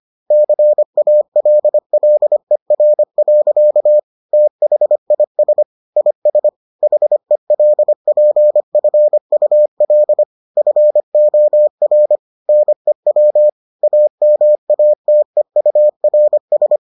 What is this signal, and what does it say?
Morse code, heard as a keyed tone, about 25 wpm, 600 Hz. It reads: CALLER. THIS IS HELPFUL FOR NEW AMATEURS